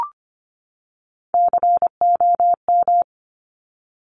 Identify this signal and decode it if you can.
Morse code, heard as a keyed tone, about 25 wpm, 700 Hz. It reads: COM
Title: Top Three Letter Combinations in Words 25wpm